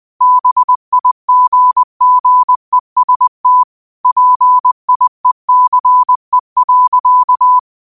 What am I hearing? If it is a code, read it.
Morse code, heard as a keyed tone, about 20 wpm, 1000 Hz. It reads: BIGGEST PIECE.